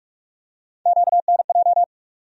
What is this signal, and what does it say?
Morse code, heard as a keyed tone, about 43 wpm, 700 Hz. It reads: QNJ
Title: USA Call Sign Postfixes - 3 Letters 45wpm